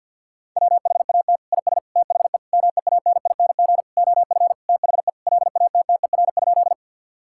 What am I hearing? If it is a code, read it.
Morse code, heard as a keyed tone, about 51 wpm, 700 Hz. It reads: WHAT IS THE MEANING OF THE LATTER?
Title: Sentences from Top 1600 Words 50wpm